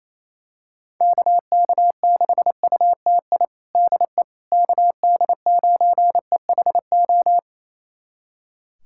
Morse code, heard as a keyed tone, about 28 wpm, 700 Hz. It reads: KK6UTI DE KD9EHO
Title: USA Callsign DE Callsign 28wpm